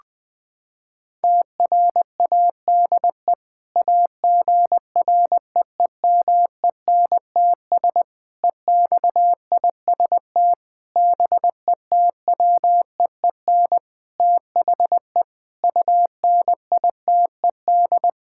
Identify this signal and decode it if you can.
Morse code, heard as a keyed tone, about 21 wpm, 700 Hz. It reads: TRADE AGREEMENTS EXIST BETWEEN THE UNITED
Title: Sentences with 5-Word Phrases - 20wpm